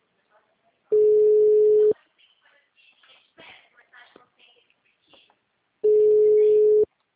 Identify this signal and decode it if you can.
Ringback tone (Europe, 425 Hz), with speech.